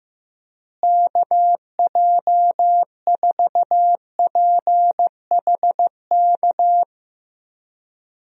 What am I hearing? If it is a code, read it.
Morse code, heard as a keyed tone, about 15 wpm, 700 Hz. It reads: KJ4PHK
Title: USA Call Signs - 6 Characters - Repeated 3x - 15wpm